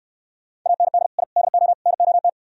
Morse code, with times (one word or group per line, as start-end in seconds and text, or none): 0.7-1.1 AND
1.2-1.2 I
1.4-1.7 DO
1.9-2.3 NOT